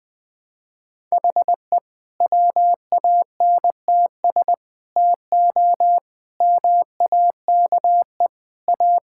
Morse code, heard as a keyed tone, about 21 wpm, 700 Hz. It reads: HE WANTS TO MAKE A
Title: Sentences with 3-Word Phrases - 20wpm